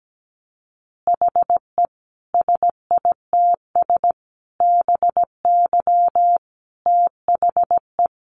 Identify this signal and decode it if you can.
Morse code, heard as a keyed tone, about 17 wpm, 700 Hz. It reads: HE SITS BY THE
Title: Additional Easy Phrases 17wpm